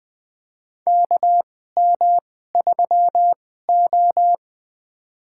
Morse code, heard as a keyed tone, about 20 wpm, spaced as for 14 wpm, 700 Hz. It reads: KM3O
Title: USA Call Signs - 4 Characters - 2x Character Spacing - 20wpm